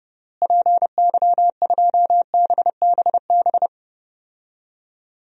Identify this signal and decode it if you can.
Morse code, heard as a keyed tone, about 32 wpm, 700 Hz. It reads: PY2BBB
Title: DXCC-100 Call Signs - 6 Characters - Repeated 2x - 30wpm